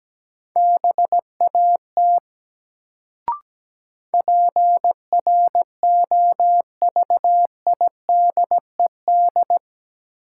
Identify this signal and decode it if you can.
Morse code, heard as a keyed tone, about 17 wpm, 700 Hz. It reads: BAT PROVIDED
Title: Sentences from Top 1500 Words - New Word Drill - 17wpm